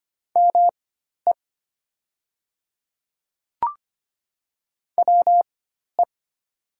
Morse code, heard as a keyed tone, about 25 wpm, spaced as for 12 wpm, 700 Hz. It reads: ME WE
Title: 4x Character Spacing - Top 200 Words - 25wpm